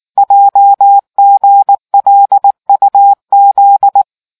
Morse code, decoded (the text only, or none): JGLUZ